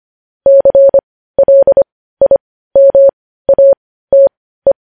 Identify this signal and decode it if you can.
Morse code, heard as a keyed tone, about 25 wpm, spaced as for 15 wpm, 550 Hz. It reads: CLIMATE
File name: podcast / News headlines-1x-25x15